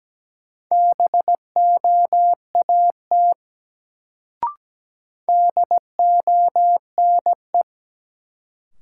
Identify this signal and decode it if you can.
Morse code, heard as a keyed tone, about 17 wpm, 700 Hz. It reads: BOAT DONE
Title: Sentences from Top 500 Words - New Word Drill - 17wpm